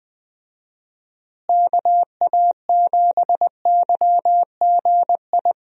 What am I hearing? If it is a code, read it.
Morse code, heard as a keyed tone, about 20 wpm, 700 Hz. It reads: KA7YGI
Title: USA Call Signs - 6 Characters - Repeated 3x - 20wpm